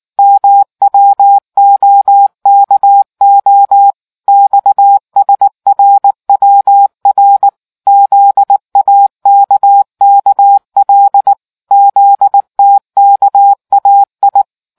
Morse code, heard as a keyed tone, about 19 wpm, 800 Hz. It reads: MWOKO XSRWR ZAKKL ZTKAI